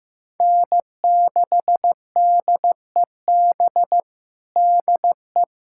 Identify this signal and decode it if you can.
Morse code, heard as a keyed tone, about 15 wpm, 700 Hz. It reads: N6DEB DE
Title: USA Callsign DE Callsign - 5 Characters - 15wpm